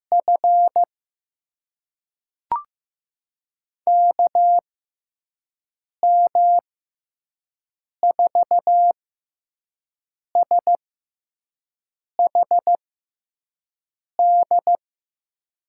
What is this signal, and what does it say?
Morse code, heard as a keyed tone, about 15 wpm, spaced as for 5 wpm, 700 Hz. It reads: F KM4SHD
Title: USA Call Signs - 6 Characters - 6x Character Spacing - 15wpm